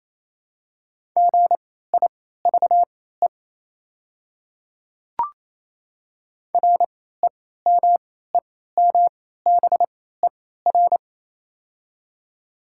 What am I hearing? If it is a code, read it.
Morse code, heard as a keyed tone, about 28 wpm, spaced as for 16 wpm, 700 Hz. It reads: GIVE REMEMBER
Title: 3x Character Spacing - Top 500 Words - 28wpm